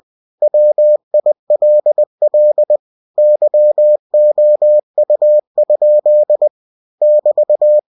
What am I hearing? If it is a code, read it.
Morse code, heard as a keyed tone, about 20 wpm, 600 Hz. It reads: WILL YOU? <BT>
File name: cw-books / winnie-the-pooh / winnie-20wpm-page-0078